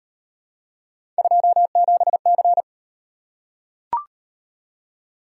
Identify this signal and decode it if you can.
Morse code, heard as a keyed tone, about 33 wpm, 700 Hz. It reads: 27C